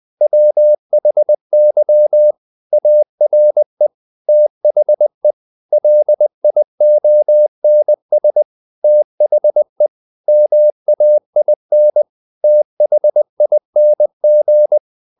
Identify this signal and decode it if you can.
Morse code, heard as a keyed tone, about 20 wpm, 600 Hz. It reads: WHY ARE THE LIONS THE MAIN THING
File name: the old man and the sea / the old man and the sea part 1 20wpm